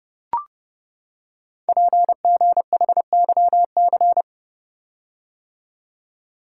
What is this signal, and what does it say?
Morse code, heard as a keyed tone, about 30 wpm, 700 Hz. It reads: PGHYC
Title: Code Groups - 5 Letters - 30wpm